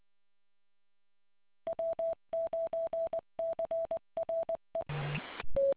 Morse code, heard as a keyed tone, about 20 wpm, 660 Hz. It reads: W9CRE